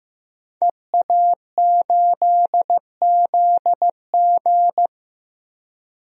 Morse code, heard as a keyed tone, about 15 wpm, 700 Hz. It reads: EA8ZG
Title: DXCC-100 Call Signs - 5 Characters - 15wpm